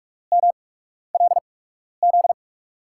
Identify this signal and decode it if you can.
Morse code, heard as a keyed tone, about 46 wpm, spaced as for 12 wpm, 700 Hz. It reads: MLZ
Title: USA Call Signs - 6 Characters - 8x Character Spacing - 45wpm